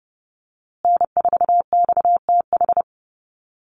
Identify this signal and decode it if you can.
Morse code, heard as a keyed tone, about 30 wpm, 700 Hz. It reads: N4XTH
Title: USA Call Signs Repeated 3x 30wpm